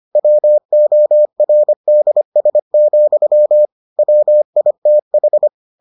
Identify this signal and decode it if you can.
Morse code, heard as a keyed tone, about 25 wpm, 600 Hz. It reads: WORDS, WITH